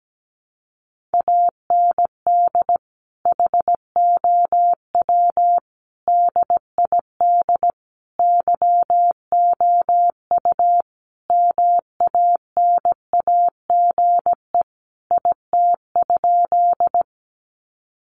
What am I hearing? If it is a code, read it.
Morse code, heard as a keyed tone, about 17 wpm, 700 Hz. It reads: AND HOW DID YOU MANAGE IT?
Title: Sentences from Top 1400 Words 17wpm